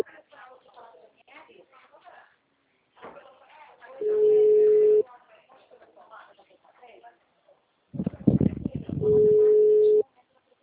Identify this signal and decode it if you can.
Ringback tone (Europe, 425 Hz), with speech.
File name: out- -2008- - - .48448